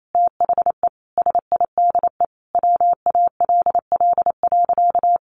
Morse code, heard as a keyed tone, about 28 wpm, 700 Hz. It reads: THE SIDE WALL.
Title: Sentences from Top 1100 Words 28wpm